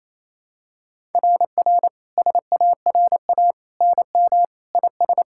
Morse code, heard as a keyed tone, about 28 wpm, 700 Hz. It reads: RR SARA NM IS
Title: SOTA QSO Practice 28wpm